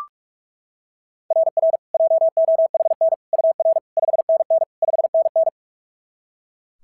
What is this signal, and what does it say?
Morse code, heard as a keyed tone, about 41 wpm, 690 Hz. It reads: RR JOHN UR 5NN 5NN
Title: SOTA QSO Practice 45wpm